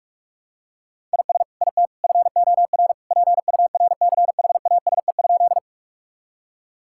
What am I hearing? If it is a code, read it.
Morse code, heard as a keyed tone, about 45 wpm, 700 Hz. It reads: IS IT FOR PURCHASE?
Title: Sentences from Top 1000 Words - Review Part 5 - 45wpm